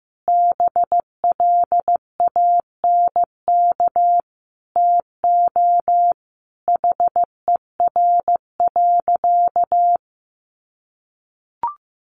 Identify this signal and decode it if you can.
Morse code, heard as a keyed tone, about 15 wpm, 700 Hz. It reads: BLANK TO HER.
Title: Sentences from Top 1500 Words 15wpm